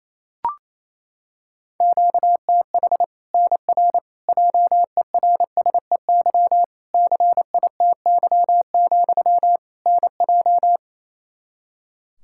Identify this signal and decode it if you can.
Morse code, heard as a keyed tone, about 27 wpm, 700 Hz. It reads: QTH NR JERSEY CITY, NJ
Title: K1USN SST QSO Practice 28wpm